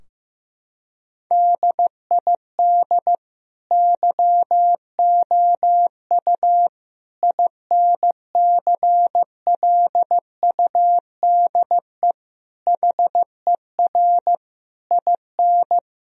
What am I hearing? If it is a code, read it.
Morse code, heard as a keyed tone, about 15 wpm, 700 Hz. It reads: DID YOU INCLUDE HER IN
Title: Sentences from Top 1000 Words - Review Part 3 - 15wpm